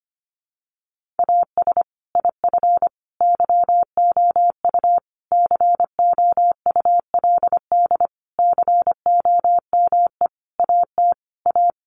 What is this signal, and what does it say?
Morse code, heard as a keyed tone, about 25 wpm, 700 Hz. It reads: AS IF YOU COULD COME AT A
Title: Sentences from Top 900 Words 25wpm